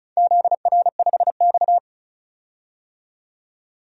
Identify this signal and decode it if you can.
Morse code, heard as a keyed tone, about 34 wpm, 700 Hz. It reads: ZR5X
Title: DXCC-100 Call Signs - 4 Characters - 35wpm